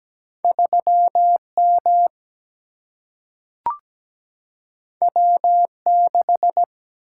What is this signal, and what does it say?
Morse code, heard as a keyed tone, about 17 wpm, 700 Hz. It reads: 3M W6
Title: Single-Letter Single-Number 17wpm